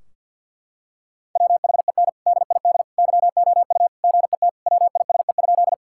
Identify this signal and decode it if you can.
Morse code, heard as a keyed tone, about 50 wpm, 700 Hz. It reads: WHEN DID YOU GET WISE?
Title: Sentences from Top 1600 Words 50wpm